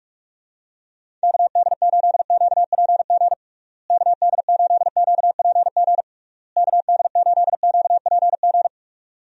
Morse code, heard as a keyed tone, about 46 wpm, 700 Hz. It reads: KD8QPG KD8QPG KD8QPG
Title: USA Call Signs - 6 Characters - Repeated 3x - 45wpm